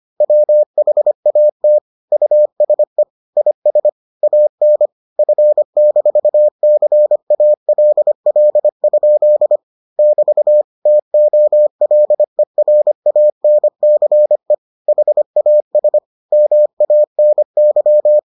Morse code, heard as a keyed tone, about 25 wpm, 600 Hz. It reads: WHAT USE IS AN F-CALL? <BT> TOLERANCE HAS MANY